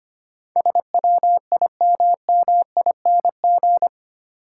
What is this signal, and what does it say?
Morse code, heard as a keyed tone, about 25 wpm, 700 Hz. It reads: SWIMMING
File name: 8 Letter Words - First 3 Letters Given - 25wpm